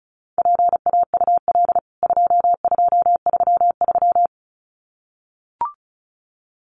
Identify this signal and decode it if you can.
Morse code, heard as a keyed tone, about 34 wpm, 700 Hz. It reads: PAUL 2233
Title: CWT Exchange 35wpm